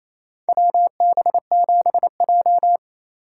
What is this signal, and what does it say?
Morse code, heard as a keyed tone, about 28 wpm, 700 Hz. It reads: WB7J
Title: USA Call Signs - 4 Characters - 3x Character Spacing - 28wpm